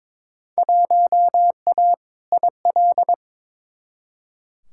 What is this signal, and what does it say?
Morse code, heard as a keyed tone, about 22 wpm, 700 Hz. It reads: 1A IL
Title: Field Day Exchanges - 22wpm